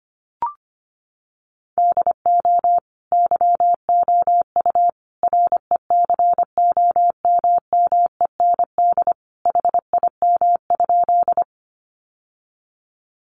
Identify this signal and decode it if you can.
Morse code, heard as a keyed tone, about 25 wpm, 700 Hz. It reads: DO YOU RECOMMEND HIM?
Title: Sentences from Top 1200 Words 25wpm